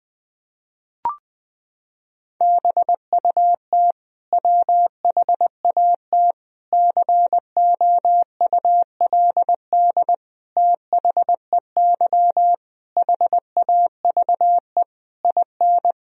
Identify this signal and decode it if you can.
Morse code, heard as a keyed tone, about 20 wpm, 700 Hz. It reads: BUT WHAT COULD THEY HAVE IN
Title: Sentences from Top 1000 Words - Review Part 3 - 20wpm